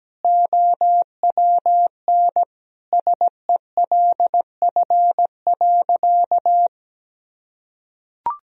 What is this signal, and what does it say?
Morse code, heard as a keyed tone, about 17 wpm, 700 Hz. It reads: OWN SELF.
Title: Sentences from Top 1000 Words - Review Part 1 - 17wpm